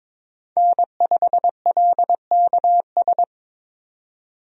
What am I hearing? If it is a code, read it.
Morse code, heard as a keyed tone, about 22 wpm, 700 Hz. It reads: N5LKS
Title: USA Call Signs - 5 Characters - 22wpm